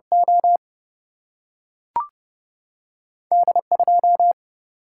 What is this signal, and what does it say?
Morse code, heard as a keyed tone, about 31 wpm, 700 Hz. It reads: O D 2
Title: Single-Letter Single-Number 30wpm